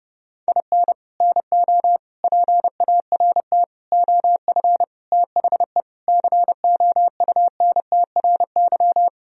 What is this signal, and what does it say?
Morse code, heard as a keyed tone, about 32 wpm, 700 Hz. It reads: IN NO PART OF THE COUNTRY
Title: Sentences from Top 1000 Words - Review Part 3 - 30wpm